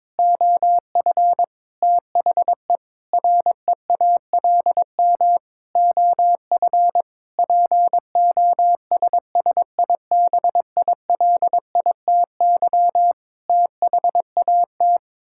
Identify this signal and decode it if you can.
Morse code, heard as a keyed tone, about 22 wpm, 700 Hz. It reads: OF THE REALM OF POSSIBILITY THAT
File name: Sets of 9 Words - 22wpm